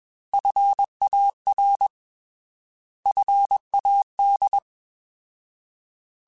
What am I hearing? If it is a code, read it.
Morse code, heard as a keyed tone, about 21 wpm, 800 Hz. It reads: FAR FAD
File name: Alphabetical / Alphabetical mp3s / Fs @20